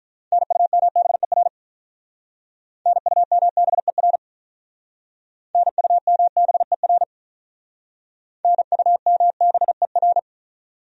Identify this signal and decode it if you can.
Morse code, heard as a keyed tone, about 45 wpm, 700 Hz. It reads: NUMBER NUMBER NUMBER NUMBER